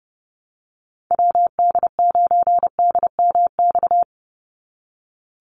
Morse code, heard as a keyed tone, about 30 wpm, 700 Hz. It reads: WD9DMX
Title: USA Call Signs 30wpm